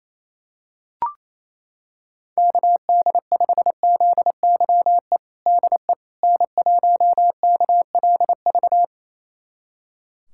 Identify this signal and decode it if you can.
Morse code, heard as a keyed tone, about 27 wpm, 700 Hz. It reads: KD5ZYE DE N1KLV
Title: USA Callsign DE Callsign 28wpm